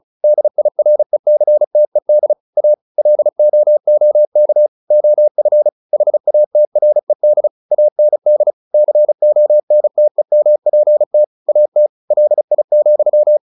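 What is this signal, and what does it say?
Morse code, heard as a keyed tone, about 34 wpm, 600 Hz. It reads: DIRECTED A LOOK OF HATRED AND CONTEMPT AT LI,
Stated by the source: Morse practice at 35 wpm from CW med Gnister, sending Ken Follett's novel Never in English.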